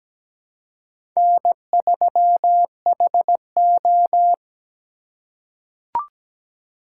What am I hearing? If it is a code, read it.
Morse code, heard as a keyed tone, about 17 wpm, 700 Hz. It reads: N3HO
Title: USA Call Signs - 4 Characters - 4x Character Spacing - 17wpm